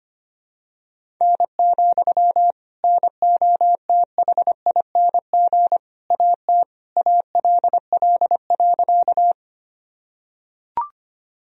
Morse code, heard as a keyed tone, about 25 wpm, 700 Hz. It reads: N, NOTHING AT ALL.
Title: Sentences from Top 500 Words and Abbreviations 25wpm